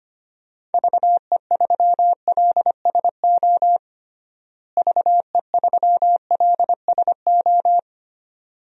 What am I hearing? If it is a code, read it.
Morse code, heard as a keyed tone, about 25 wpm, 700 Hz. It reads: VE3LSO VE3LSO
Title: DXCC-100 Call Signs - 6 Characters - Repeated 2x - 25wpm